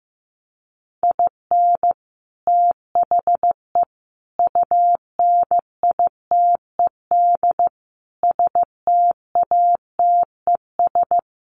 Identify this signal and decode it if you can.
Morse code, heard as a keyed tone, about 15 wpm, 700 Hz. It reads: IN THE UNITED STATES
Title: Sets of 5 Words Encore2 15wpm